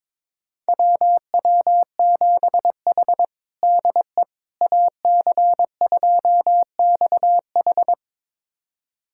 Morse code, heard as a keyed tone, about 21 wpm, 700 Hz. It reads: WW7H DE AC2XH
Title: USA Callsign DE Callsign 22wpm